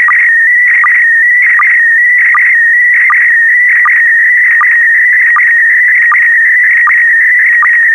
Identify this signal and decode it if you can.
SSTV (slow-scan television), PD 180.
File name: space comms pd180 sstv test recording1